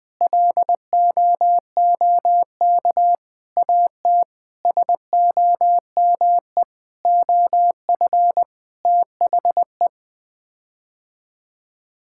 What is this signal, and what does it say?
Morse code, heard as a keyed tone, about 20 wpm, 700 Hz. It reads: LOOK AT SOME OF THE
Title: Sets of 8 Words - 20wpm